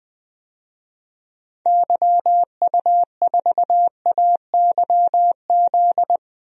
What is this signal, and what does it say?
Morse code, heard as a keyed tone, about 20 wpm, 700 Hz. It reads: YU4AYZ
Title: DXCC-100 Call Signs - 6 Characters - Repeated 3x - 20wpm